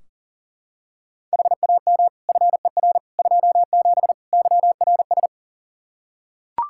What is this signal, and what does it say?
Morse code, heard as a keyed tone, about 37 wpm, 700 Hz. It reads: HAM FER 27 YRS
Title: Practice QSO - Exchange 3 - More Info - 40wpm